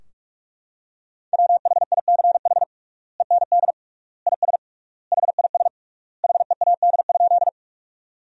Morse code, heard as a keyed tone, about 46 wpm, 700 Hz. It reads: WHICH END IS HIS HEAD?
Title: 3x Word-Spacing - Sentences from Top 300 Words - 45wpm